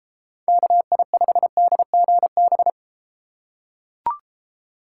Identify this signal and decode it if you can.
Morse code, heard as a keyed tone, about 33 wpm, 700 Hz. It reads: KI5DGB